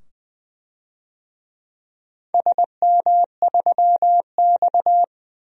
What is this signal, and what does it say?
Morse code, heard as a keyed tone, about 20 wpm, 700 Hz. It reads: SM3X
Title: DXCC-100 Call Signs - 4 Characters - 20wpm